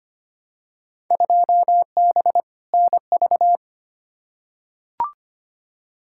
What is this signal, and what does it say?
Morse code, heard as a keyed tone, about 25 wpm, 700 Hz. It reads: 2B NV
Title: Field Day Exchanges Encore - 25wpm